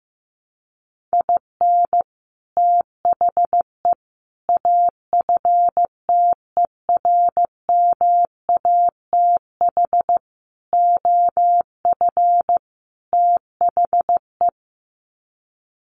Morse code, heard as a keyed tone, about 15 wpm, 700 Hz. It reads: IN THE AFTERMATH OF THE
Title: Sets of 5 Words 15wpm